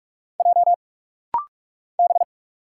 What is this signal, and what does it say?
Morse code, heard as a keyed tone, about 46 wpm, 700 Hz. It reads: J B
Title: Single Letters - Rapid-Fire 45wpm